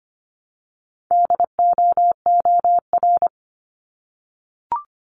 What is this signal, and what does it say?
Morse code, heard as a keyed tone, about 25 wpm, 700 Hz. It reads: DOOR
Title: Multi-letter Phonemes Repeated 3x 25wpm